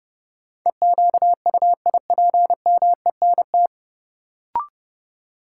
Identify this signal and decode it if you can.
Morse code, heard as a keyed tone, about 29 wpm, 700 Hz. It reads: EQUIPMENT
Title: Sentences from Top 700 Words - New Word Drill - 30wpm